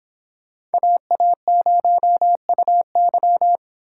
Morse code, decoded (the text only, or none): AA0UY